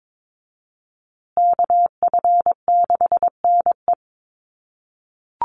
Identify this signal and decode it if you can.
Morse code, heard as a keyed tone, about 22 wpm, 700 Hz. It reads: KF6NE
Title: USA Call Signs Repeated 3x 22wpm